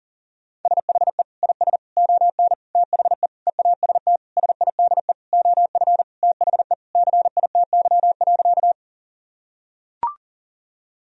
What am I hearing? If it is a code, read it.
Morse code, heard as a keyed tone, about 38 wpm, 700 Hz. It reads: SHE IS ON THE EAST SIDE OF THE CITY.
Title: Sentences from Top 1000 Words - Review Part 3 - 40wpm